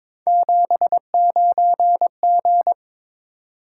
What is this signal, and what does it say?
Morse code, heard as a keyed tone, about 22 wpm, 700 Hz. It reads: 79G
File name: practice / 22wpm / Three Letter-Number Combinations - 22wpm